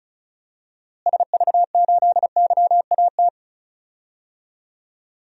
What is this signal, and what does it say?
Morse code, heard as a keyed tone, about 36 wpm, 700 Hz. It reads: SV8YAT
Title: DXCC-100 Call Signs - 6 Characters - Repeated 2x - 35wpm